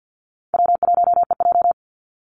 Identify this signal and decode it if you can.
Morse code, heard as a keyed tone, about 50 wpm, 700 Hz. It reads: W1EJ
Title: NA SOTA Activator Callsigns 50wpm